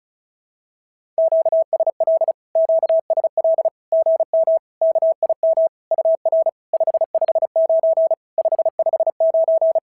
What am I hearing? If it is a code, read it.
Morse code, heard as a keyed tone, about 36 wpm, 650 Hz. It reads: QSL QSL GM KIM UR 559 559
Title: SOTA QSO Practice 35wpm